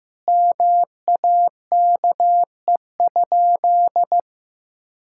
Morse code, heard as a keyed tone, about 16 wpm, 700 Hz. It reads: MAKE?
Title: Sentences from Top 1000 Words - Review Part 5 - 15wpm